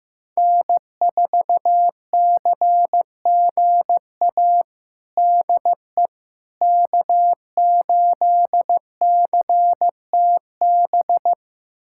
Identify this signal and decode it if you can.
Morse code, heard as a keyed tone, about 15 wpm, 700 Hz. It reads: N4CGA DE K8CTB
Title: USA Callsign DE Callsign - 5 Characters - 15wpm